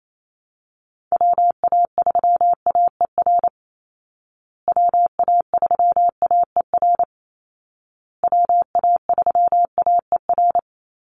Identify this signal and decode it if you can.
Morse code, heard as a keyed tone, about 28 wpm, 700 Hz. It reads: WA3AER WA3AER WA3AER
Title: Call Signs Repeated 2x - 28wpm